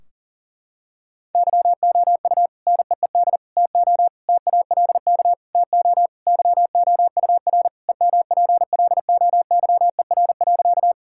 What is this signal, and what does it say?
Morse code, heard as a keyed tone, about 40 wpm, 700 Hz. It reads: YOU NEED TO TALK TO YOUR EMPLOYER.
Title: Sentences from Top 1600 Words 40wpm